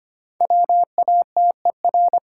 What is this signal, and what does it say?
Morse code, heard as a keyed tone, about 25 wpm, 700 Hz. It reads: WATER
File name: practice / 25wpm / Difficult Words - Speed-Racing with Hints - 25wpm